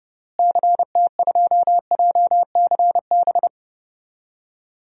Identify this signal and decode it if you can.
Morse code, heard as a keyed tone, about 32 wpm, 700 Hz. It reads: CT2JCB
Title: DXCC-100 Call Signs - 6 Characters - Repeated 2x - 30wpm